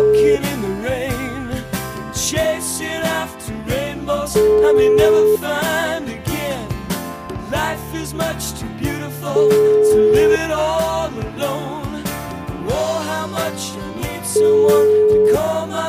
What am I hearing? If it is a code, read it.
Ringback tone (Europe, 425 Hz), with noise.